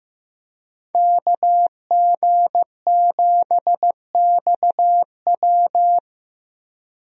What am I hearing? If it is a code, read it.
Morse code, heard as a keyed tone, about 15 wpm, 700 Hz. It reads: KG7XW
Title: USA Call Signs - 5 Characters - 15wpm